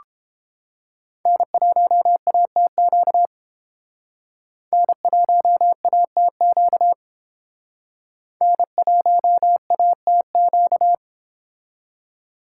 Morse code, heard as a keyed tone, about 29 wpm, 700 Hz. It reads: N1ATQ N1ATQ N1ATQ